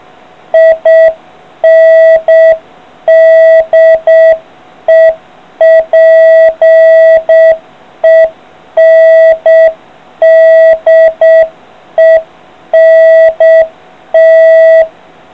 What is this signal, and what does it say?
Morse code, heard as a keyed tone, about 7 wpm, 650 Hz. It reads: INDEPENDENT